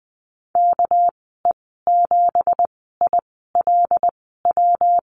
Morse code, heard as a keyed tone, about 20 wpm, spaced as for 15 wpm, 700 Hz. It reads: KE7ILW
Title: 200 Lesson 32g - Letter k - Call Signs 20wpm